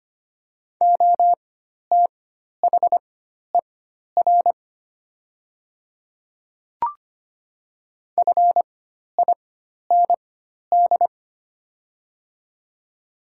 Morse code, heard as a keyed tone, about 25 wpm, spaced as for 12 wpm, 700 Hz. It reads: OTHER FIND
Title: 4x Character Spacing - Top 100 Words - 25wpm